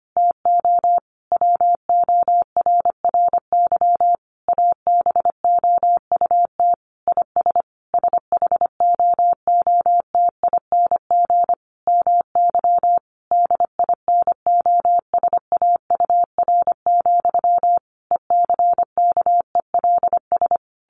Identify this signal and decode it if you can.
Morse code, heard as a keyed tone, about 25 wpm, 700 Hz. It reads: TO WORRY ABOUT IS SHOOTING MY DINOSAUR, ECKELS
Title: A Sound Of Thunder 25wpm